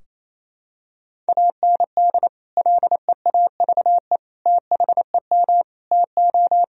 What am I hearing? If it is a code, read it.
Morse code, heard as a keyed tone, about 27 wpm, 700 Hz. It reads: AND LEAVE THEM TO
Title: Sentences from Top 1000 Words - Review Part 2 - 28wpm